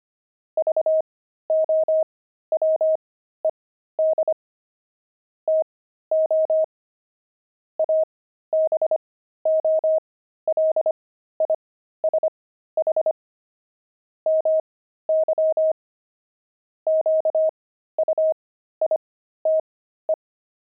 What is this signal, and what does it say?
Morse code, heard as a keyed tone, about 25 wpm, spaced as for 13 wpm, 630 Hz. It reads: VOWED TO ABOLISH MY QUITE
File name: pangram-fs-13-cs-25-crazy